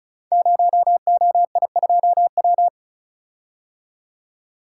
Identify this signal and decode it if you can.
Morse code, heard as a keyed tone, about 36 wpm, 700 Hz. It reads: 0OI2W
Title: Code Groups - 5 Letter:Numbers - 35wpm